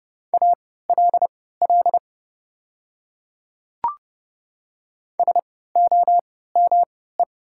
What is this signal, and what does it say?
Morse code, heard as a keyed tone, about 29 wpm, spaced as for 17 wpm, 690 Hz. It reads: ALL SOME
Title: 3x Character Spacing - Top 100 Words - 30wpm